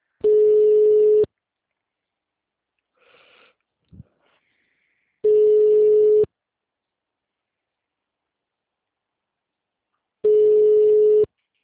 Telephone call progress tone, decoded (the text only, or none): ringback tone